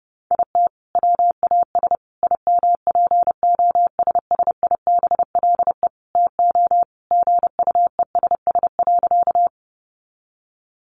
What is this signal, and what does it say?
Morse code, heard as a keyed tone, about 30 wpm, 700 Hz. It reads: IT WAS IMPOSSIBLE TO GUESS.
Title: Sentences from Top 1500 Words 30wpm